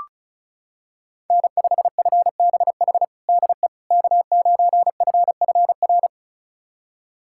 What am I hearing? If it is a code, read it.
Morse code, heard as a keyed tone, about 36 wpm, 700 Hz. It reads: N5FBH DE K9FFR
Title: USA Callsign DE Callsign - 5 Characters - 35wpm